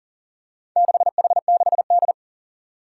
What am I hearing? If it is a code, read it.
Morse code, heard as a keyed tone, about 42 wpm, 700 Hz. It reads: 6H6D
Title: Four Challenging Letter-Number Combinations 40wpm